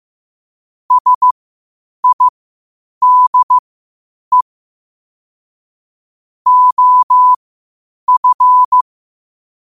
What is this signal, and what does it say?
Morse code, heard as a keyed tone, about 15 wpm, spaced as for 8 wpm, 1000 Hz. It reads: SIDE OF
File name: qotdfiles / 8wpm